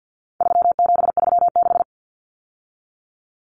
Morse code, heard as a keyed tone, about 48 wpm, 700 Hz. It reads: 3736
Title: Numbers - 4-Digit 50wpm